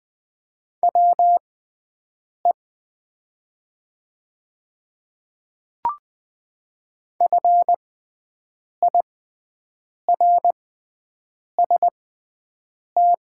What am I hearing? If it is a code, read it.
Morse code, heard as a keyed tone, about 20 wpm, spaced as for 7 wpm, 700 Hz. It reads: WE FIRST
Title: 6x Character Spacing - Top 100 Words - 20wpm